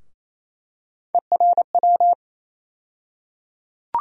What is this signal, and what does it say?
Morse code, heard as a keyed tone, about 27 wpm, 700 Hz. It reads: ERW
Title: USA Call Sign Postfixes - 3 Letters 28wpm